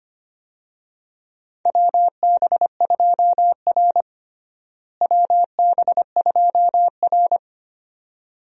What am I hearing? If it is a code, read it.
Morse code, heard as a keyed tone, about 25 wpm, 700 Hz. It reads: WB2R WB2R
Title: USA Call Signs - 4 Characters - Repeated 2x - 25wpm